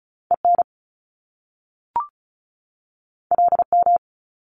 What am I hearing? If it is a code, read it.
Morse code, heard as a keyed tone, about 36 wpm, 700 Hz. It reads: EN LM
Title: Top Two Letter Combinations in Words 35wpm